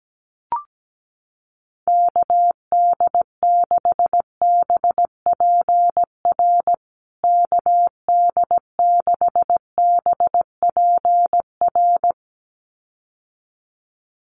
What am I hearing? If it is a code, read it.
Morse code, heard as a keyed tone, about 17 wpm, 700 Hz. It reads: KD6BPR KD6BPR
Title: USA Call Signs Repeated 2x 17wpm